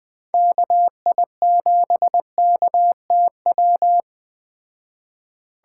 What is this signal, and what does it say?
Morse code, heard as a keyed tone, about 20 wpm, 700 Hz. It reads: KI7KTW